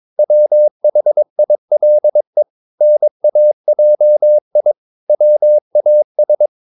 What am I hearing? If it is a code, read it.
Morse code, heard as a keyed tone, about 22 wpm, 600 Hz. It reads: WHILE NAJI WAS
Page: CW med Gnister